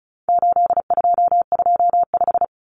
Morse code, heard as a keyed tone, about 34 wpm, 700 Hz. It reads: 8225
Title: Numbers - 4-Digit 35wpm